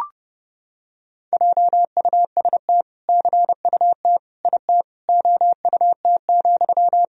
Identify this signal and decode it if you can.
Morse code, heard as a keyed tone, about 30 wpm, 700 Hz. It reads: JUST CUT IT OUT,
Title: Sentences from Top 300 Words and Abbreviations 30wpm